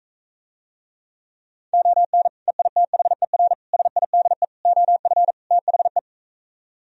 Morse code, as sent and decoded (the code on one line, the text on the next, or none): --- -. / . .. - .... .-. / ... .. -.. / --- ..-. / - .... .
ON EITHER SIDE OF THE